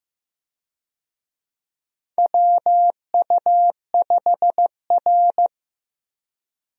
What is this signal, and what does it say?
Morse code, heard as a keyed tone, about 15 wpm, 700 Hz. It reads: WU5R
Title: USA Call Signs - 4 Characters - Repeated 3x - 15wpm